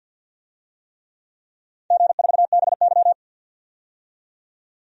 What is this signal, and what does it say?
Morse code, heard as a keyed tone, about 52 wpm, 700 Hz. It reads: G4BY
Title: DXCC-100 Call Signs - 4 Characters - 50wpm